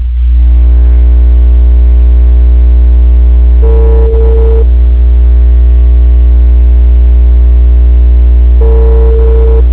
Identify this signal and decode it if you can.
Ringback tone (China, 450 Hz), with noise.
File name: out- -2019- - - .71109